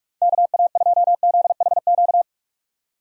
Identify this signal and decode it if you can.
Morse code, heard as a keyed tone, about 41 wpm, 700 Hz. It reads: KA2ZHQ
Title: USA Call Signs - 6 Characters - 45wpm